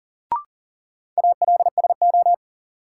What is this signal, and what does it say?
Morse code, heard as a keyed tone, about 40 wpm, 700 Hz. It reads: ALSO
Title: Sentences from Top 400 Words - New Word Drill - 40wpm